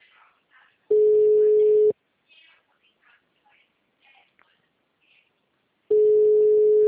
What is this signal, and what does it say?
Ringback tone (Europe, 425 Hz).